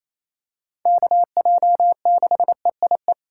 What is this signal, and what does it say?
Morse code, heard as a keyed tone, about 28 wpm, 690 Hz. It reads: KJ6EIE
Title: USA Call Signs - 6 Characters - 4x Character Spacing - 28wpm